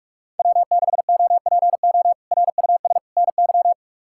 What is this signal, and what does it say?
Morse code, heard as a keyed tone, about 42 wpm, 700 Hz. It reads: W6OPO RUS NY
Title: K1USN SST QSO Practice - 45wpm